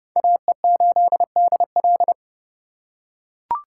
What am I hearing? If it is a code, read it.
Morse code, heard as a keyed tone, about 29 wpm, 700 Hz. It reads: AE8DL
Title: USA Call Signs - 4x Character Spacing - 30wpm